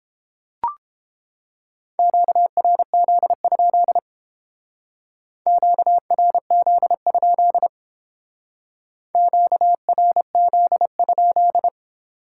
Morse code, heard as a keyed tone, about 31 wpm, 700 Hz. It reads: QRZ? QRZ? QRZ?